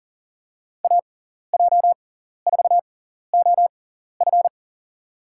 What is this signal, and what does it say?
Morse code, heard as a keyed tone, about 40 wpm, spaced as for 14 wpm, 700 Hz. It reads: AJ4OF
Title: USA Call Signs - 5 Characters - 6x Character Spacing - 40wpm